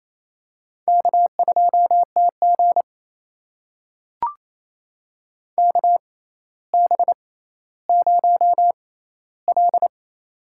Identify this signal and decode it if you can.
Morse code, heard as a keyed tone, about 28 wpm, spaced as for 14 wpm, 700 Hz. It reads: K2TG KB0L
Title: USA Call Signs - 4 Characters - 6x Character Spacing - 28wpm